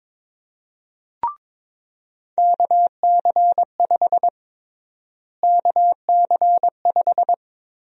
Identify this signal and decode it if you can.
Morse code, heard as a keyed tone, about 21 wpm, 700 Hz. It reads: KC5 KC5
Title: USA Call Sign Prefixes - Repeated 2x - 22wpm